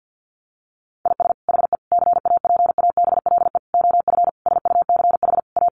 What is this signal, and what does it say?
Morse code, heard as a keyed tone, about 49 wpm, 700 Hz. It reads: IS HE CAPABLE OF SUCH A
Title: Sentences from Top 1200 Words 50wpm